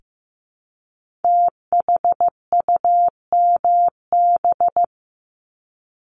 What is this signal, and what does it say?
Morse code, heard as a keyed tone, about 15 wpm, 700 Hz. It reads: THUMB
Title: Multi-letter Phonemes 15wpm